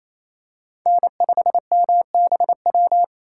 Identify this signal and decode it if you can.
Morse code, heard as a keyed tone, about 28 wpm, 700 Hz. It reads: N5MBW
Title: USA Call Signs - 5 Characters - 2x Character Spacing - 28wpm